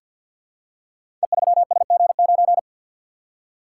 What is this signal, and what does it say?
Morse code, heard as a keyed tone, about 46 wpm, 700 Hz. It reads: E3SG9
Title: Code Groups - 5 Letter:Numbers - 50wpm